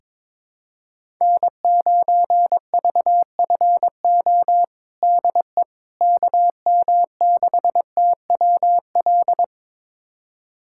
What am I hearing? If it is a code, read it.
Morse code, heard as a keyed tone, about 22 wpm, 700 Hz. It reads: N9VFO DE KM6TWL
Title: USA Callsign DE Callsign 22wpm